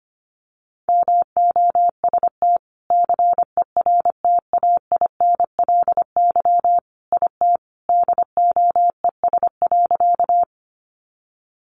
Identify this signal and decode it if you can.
Morse code, heard as a keyed tone, about 25 wpm, 700 Hz. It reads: MOST CERTAINLY IT DOES.
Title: Sentences from Top 900 Words 25wpm